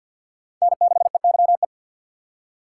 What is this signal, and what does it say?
Morse code, heard as a keyed tone, about 51 wpm, 700 Hz. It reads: N6EYE
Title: USA Call Signs - 5 Characters - 3x Character Spacing - 50wpm